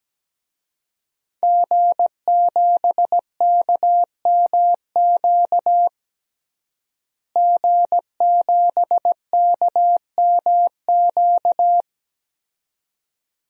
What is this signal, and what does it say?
Morse code, heard as a keyed tone, about 17 wpm, 700 Hz. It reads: G7KMQ G7KMQ
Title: DXCC-100 Call Signs - 5 Characters - Repeated 2x - 17wpm